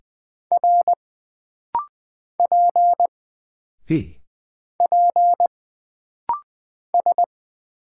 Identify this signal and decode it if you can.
Morse code, heard as a keyed tone, about 20 wpm, 700 Hz. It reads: R P P S
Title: Single Character (Letter:Number:Punctuation:Prosign) - Rapid-Fire - 20wpm